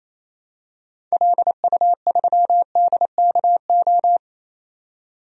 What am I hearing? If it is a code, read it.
Morse code, heard as a keyed tone, about 27 wpm, 700 Hz. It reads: LU3DKO
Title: DXCC-100 Call Signs - 6 Characters - 28wpm